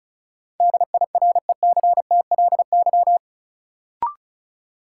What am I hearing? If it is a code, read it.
Morse code, heard as a keyed tone, about 36 wpm, 700 Hz. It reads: DIRECTLY
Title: Sentences from Top 800 Words - New Word Drill - 35wpm